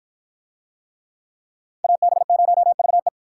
Morse code, heard as a keyed tone, about 52 wpm, 700 Hz. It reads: AB0VE